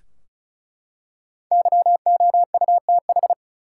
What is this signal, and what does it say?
Morse code, heard as a keyed tone, about 35 wpm, 700 Hz. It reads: YOUTH
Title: Five Letter Words - 35wpm